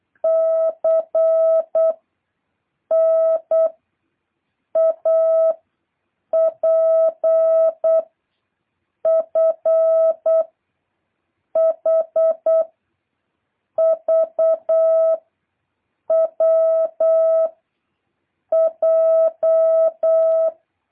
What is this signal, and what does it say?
Morse code, heard as a keyed tone, about 8 wpm, 650 Hz. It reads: C N A P F H V W J
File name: jh8cbh / 2023cw